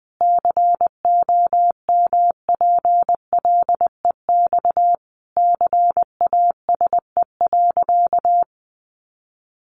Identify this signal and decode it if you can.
Morse code, heard as a keyed tone, about 20 wpm, 700 Hz. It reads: COMPLEX CASE.
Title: Sentences from Top 900 Words 20wpm